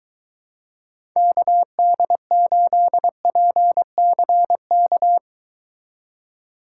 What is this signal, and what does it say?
Morse code, heard as a keyed tone, about 23 wpm, 700 Hz. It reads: KD8PCK